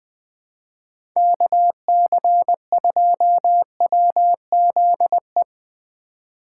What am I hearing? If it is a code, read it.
Morse code, heard as a keyed tone, about 21 wpm, 700 Hz. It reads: KC2WZE
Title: Speed Racing - USA 2x3 Call Signs 17wpm